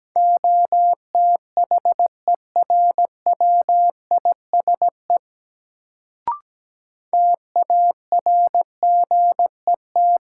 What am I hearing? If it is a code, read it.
Morse code, heard as a keyed tone, about 17 wpm, 700 Hz. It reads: OTHERWISE TARGET
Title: Sentences from Top 1200 Words - New Word Drill Alt - 17wpm